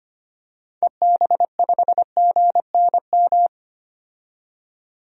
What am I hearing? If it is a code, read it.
Morse code, heard as a keyed tone, about 25 wpm, 700 Hz. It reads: EB5GNM
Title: DXCC-100 Call Signs - 6 Characters - 25wpm